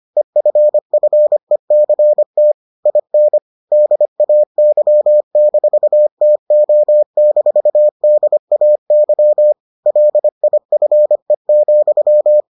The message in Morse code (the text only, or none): EFFECT IN DAY-TO-DAY LIFE,